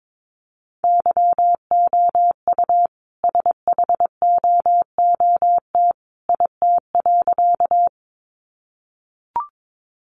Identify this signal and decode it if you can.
Morse code, heard as a keyed tone, about 22 wpm, 700 Hz. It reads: YOU SHOOT IT.
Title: A Sound Of Thunder 22wpm